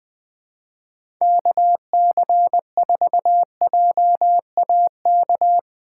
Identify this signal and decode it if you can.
Morse code, heard as a keyed tone, about 20 wpm, 700 Hz. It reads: KC4JAK
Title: USA Call Signs - 4x Character Spacing - 20wpm